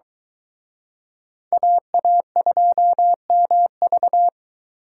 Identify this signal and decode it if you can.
Morse code, heard as a keyed tone, about 23 wpm, 700 Hz. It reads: AA2MV